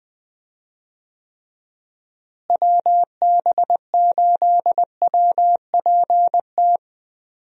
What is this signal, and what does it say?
Morse code, heard as a keyed tone, about 20 wpm, 700 Hz. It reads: WB8WPT
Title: USA Call Signs - 6 Characters - Repeated 3x - 20wpm